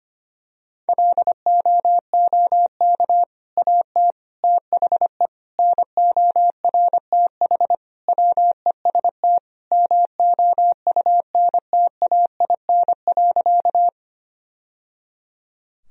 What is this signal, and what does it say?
Morse code, heard as a keyed tone, about 25 wpm, 700 Hz. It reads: LOOK AT THE NORTH WEST MOUNTAIN.
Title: Sentences from Top 1000 Words - Review Part 2 - 25wpm